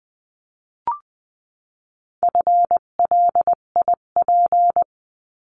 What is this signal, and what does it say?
Morse code, heard as a keyed tone, about 20 wpm, 700 Hz. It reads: FLIP
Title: 258 Lesson 42c - Words 16-20wpm